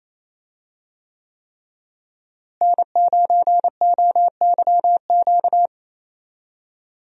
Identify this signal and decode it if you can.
Morse code, heard as a keyed tone, about 28 wpm, 700 Hz. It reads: N9OYQ
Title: USA Call Signs - 5 Characters - Repeated 3x - 28wpm